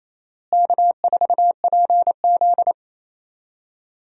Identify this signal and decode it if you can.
Morse code, heard as a keyed tone, about 28 wpm, 700 Hz. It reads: K4PZ
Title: USA Call Signs - 4 Characters - Repeated 3x - 28wpm